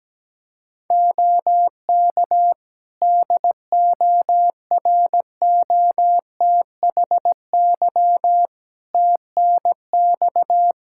Morse code, heard as a keyed tone, about 17 wpm, 700 Hz. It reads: OK DOROTHY TNX
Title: Practice QSO - Exchange 4 - Ending - 17wpm